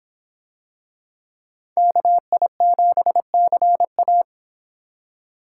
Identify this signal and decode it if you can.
Morse code, heard as a keyed tone, about 25 wpm, 700 Hz. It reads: KI7CA